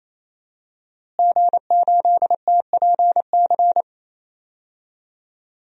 Morse code, heard as a keyed tone, about 28 wpm, 700 Hz. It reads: G8TPC
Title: DXCC-100 Call Signs - 5 Characters - Repeated 3x - 28wpm